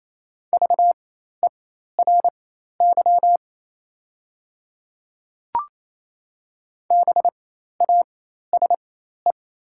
Morse code, heard as a keyed tone, about 28 wpm, spaced as for 13 wpm, 700 Hz. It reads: VERY BASE
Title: 4x Character Spacing - Top 300 Words - 28wpm